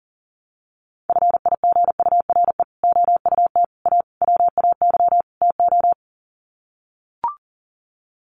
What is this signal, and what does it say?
Morse code, heard as a keyed tone, about 40 wpm, 700 Hz. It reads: FIGURE OUT A WAY TO
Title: Sets of 5 Words Encore2 40wpm